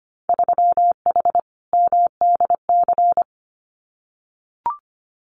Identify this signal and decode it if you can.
Morse code, heard as a keyed tone, about 25 wpm, 700 Hz. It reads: 3H MDC
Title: Winter Field Day Exchanges Encore 25wpm